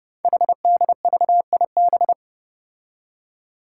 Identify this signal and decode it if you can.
Morse code, heard as a keyed tone, about 28 wpm, 690 Hz. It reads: HDVIB
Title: Code Groups - 5 Letters - 30wpm